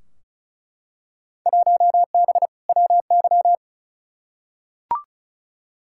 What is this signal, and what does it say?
Morse code, heard as a keyed tone, about 36 wpm, 700 Hz. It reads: 1B WY
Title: Field Day Exchanges Encore - 35wpm